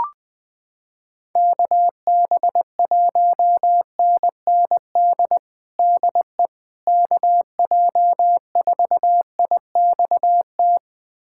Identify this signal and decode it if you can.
Morse code, heard as a keyed tone, about 20 wpm, 700 Hz. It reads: KB1NND DE KJ4IXT
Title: USA Callsign DE Callsign - 6 Characters - 20wpm